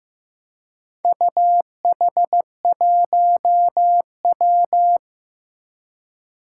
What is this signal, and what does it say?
Morse code, heard as a keyed tone, about 16 wpm, 700 Hz. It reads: UH1W
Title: Four Challenging Letter-Number Combinations 15wpm